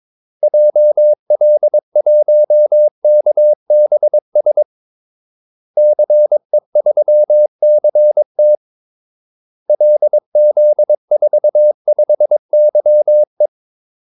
Morse code, heard as a keyed tone, about 22 wpm, 600 Hz. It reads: JL1KBS CE3CT LZ45YE
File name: Callsigns-22wpm